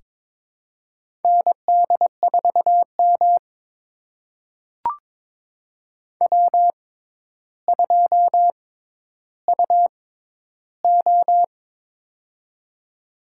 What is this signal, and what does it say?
Morse code, heard as a keyed tone, about 22 wpm, spaced as for 11 wpm, 700 Hz. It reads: ND4M W2UO
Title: USA Call Signs - 6x Character Spacing - 22wpm